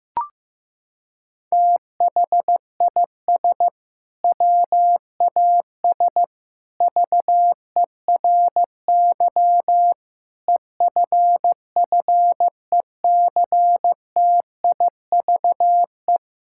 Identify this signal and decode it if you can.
Morse code, heard as a keyed tone, about 15 wpm, 700 Hz. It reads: THIS WAS VERY EFFECTIVE
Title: Sentences from Top 1000 Words - Review Part 6 - 15wpm